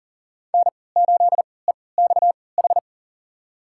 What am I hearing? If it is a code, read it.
Morse code, heard as a keyed tone, about 40 wpm, spaced as for 23 wpm, 700 Hz. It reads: N8EXH
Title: USA Call Signs - 3x Character Spacing - 40wpm